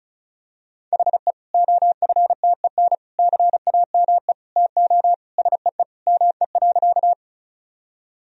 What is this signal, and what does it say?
Morse code, heard as a keyed tone, about 36 wpm, 700 Hz. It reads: HE OFTEN CAME TO SEE ME.
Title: Sentences from Top 1000 Words - Review Part 1 - 35wpm